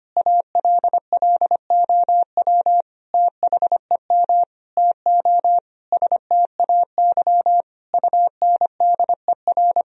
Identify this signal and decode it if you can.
Morse code, heard as a keyed tone, about 25 wpm, 700 Hz. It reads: ALLOW THEM TO STAY UNDER
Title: Sentences from Top 1000 Words - Review Part 2 - 25wpm